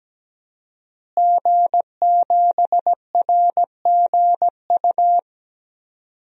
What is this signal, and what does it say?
Morse code, heard as a keyed tone, about 17 wpm, 700 Hz. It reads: G7RGU
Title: DXCC-100 Call Signs - 5 Characters - Repeated 3x - 17wpm